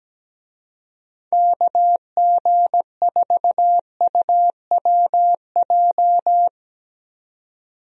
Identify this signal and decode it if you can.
Morse code, heard as a keyed tone, about 17 wpm, 700 Hz. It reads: KG4UWJ
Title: USA Call Signs - 6 Characters - 17wpm